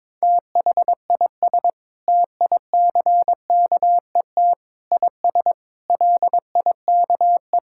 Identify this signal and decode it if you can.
Morse code, heard as a keyed tone, about 22 wpm, 700 Hz. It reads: THIS TICKET IS LIKE
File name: Sentences with 7-Word Phrases - 22wpm